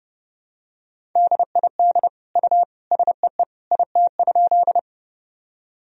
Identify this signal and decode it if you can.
Morse code, heard as a keyed tone, about 30 wpm, 700 Hz. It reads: DID U SEE IT?
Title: Sentences from Top 100 Words and Abbreviations 30wpm